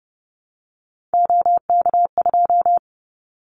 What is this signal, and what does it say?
Morse code, heard as a keyed tone, about 29 wpm, 700 Hz. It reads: OK2
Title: Callsign Prefixes for Top 25 DXCC - 30wpm